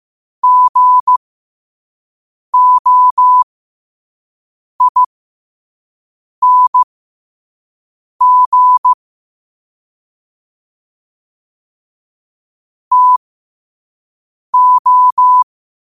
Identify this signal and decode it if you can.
Morse code, heard as a keyed tone, about 15 wpm, spaced as for 5 wpm, 1000 Hz. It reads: GOING TO
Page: 5 WPM morse code quotes for Wed, 30 Jul 2025.